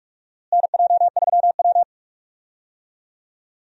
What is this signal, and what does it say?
Morse code, heard as a keyed tone, about 46 wpm, 700 Hz. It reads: NJ3W
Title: USA Call Signs - 4 Characters - Repeated 2x - 45wpm